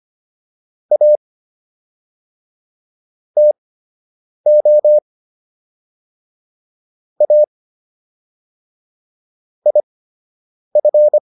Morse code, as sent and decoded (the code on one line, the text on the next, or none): .- / - --- / .- / .. ..-.
A TO A IF